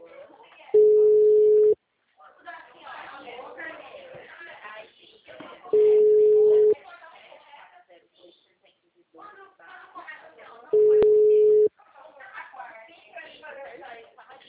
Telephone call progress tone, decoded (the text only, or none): ringback tone